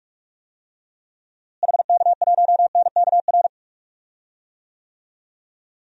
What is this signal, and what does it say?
Morse code, heard as a keyed tone, about 44 wpm, 700 Hz. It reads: HK1NKR
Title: DXCC-100 Call Signs - 6 Characters - Repeated 2x - 45wpm